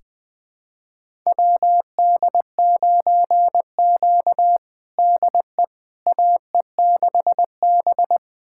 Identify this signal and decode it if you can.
Morse code, heard as a keyed tone, about 20 wpm, 700 Hz. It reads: WD9Q DE AE6B
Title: USA Callsign DE Callsign - 4 Characters - 20wpm